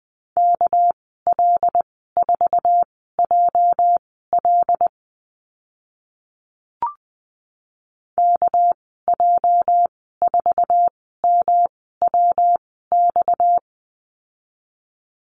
Morse code, heard as a keyed tone, about 20 wpm, spaced as for 14 wpm, 700 Hz. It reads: KL4JL KJ4MWX
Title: 231 Lesson 37g - Letter x - Call Signs 20wpm